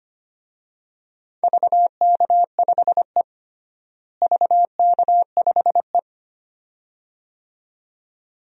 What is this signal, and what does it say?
Morse code, heard as a keyed tone, about 25 wpm, 700 Hz. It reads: VK5E VK5E
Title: DXCC-100 Call Signs - 4 Characters - Repeated 2x - 25wpm